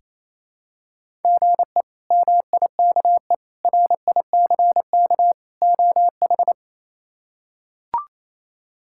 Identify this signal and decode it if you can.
Morse code, heard as a keyed tone, about 27 wpm, 700 Hz. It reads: GE MIKE RICK OH
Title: K1USN SST QSO Practice - 28wpm